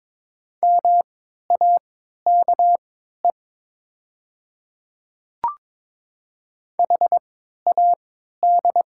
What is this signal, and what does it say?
Morse code, heard as a keyed tone, about 22 wpm, spaced as for 12 wpm, 700 Hz. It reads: MAKE HAD
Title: 3x Character Spacing - Top 300 Words - 22wpm